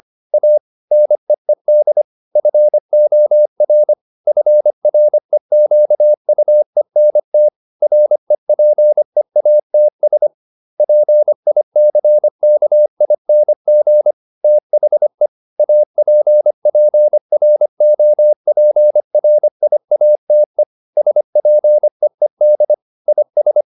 Morse code, heard as a keyed tone, about 25 wpm, 600 Hz. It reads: A NEED FOR FREQUENT REPEATS PICKING THE APPROPRIATE SPEED IS